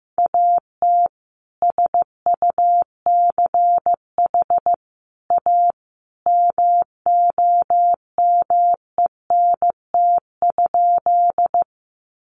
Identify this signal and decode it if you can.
Morse code, heard as a keyed tone, about 15 wpm, 700 Hz. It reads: AT SUCH A MOMENT?
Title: Sentences from Top 1000 Words 15wpm